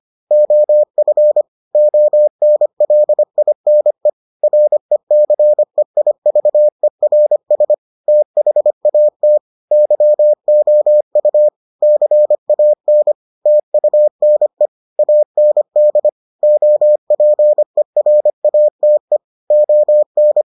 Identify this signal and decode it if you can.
Morse code, heard as a keyed tone, about 25 wpm, 600 Hz. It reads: OF ONLINE RECEIVERS THAT YOU CAN TUNE AND OPERATE ON